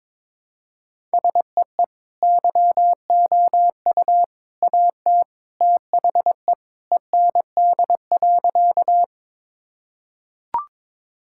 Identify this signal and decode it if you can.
Morse code, heard as a keyed tone, about 22 wpm, 700 Hz. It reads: SEE YOU AT THE END.
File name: practice / 22wpm / Sentences with 3-Word Phrases - 22wpm